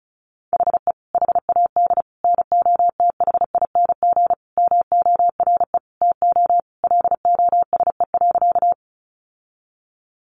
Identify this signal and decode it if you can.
Morse code, heard as a keyed tone, about 34 wpm, 700 Hz. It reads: HE HAD NOTHING MORE TO LOSE.
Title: Sentences from Top 1100 Words 35wpm